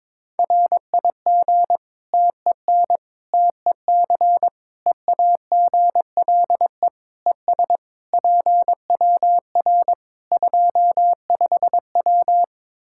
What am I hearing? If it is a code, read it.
Morse code, heard as a keyed tone, about 22 wpm, 700 Hz. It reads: RIG TEN TEC EAGLE ES PWR 25W
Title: K1USN SST QSO Practice 22wpm